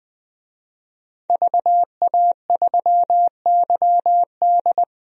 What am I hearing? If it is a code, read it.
Morse code, heard as a keyed tone, about 21 wpm, 700 Hz. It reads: VA3YD
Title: DXCC-100 Call Signs - 5 Characters - Repeated 2x - 20wpm